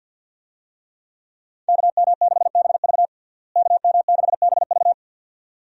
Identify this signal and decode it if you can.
Morse code, heard as a keyed tone, about 52 wpm, 700 Hz. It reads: KM6BV KM6BV
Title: USA Call Signs - 5 Characters - Repeated 2x - 50wpm